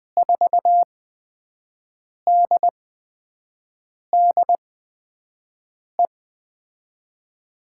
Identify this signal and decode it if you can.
Morse code, heard as a keyed tone, about 20 wpm, spaced as for 5 wpm, 700 Hz. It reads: 4DDE
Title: USA Call Signs - 6 Characters - 8x Character Spacing - 20wpm